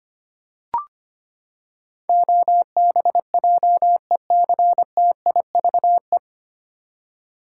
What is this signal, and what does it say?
Morse code, heard as a keyed tone, about 25 wpm, 700 Hz. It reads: OBJECTIVE
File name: Nine Letter Words - 25wpm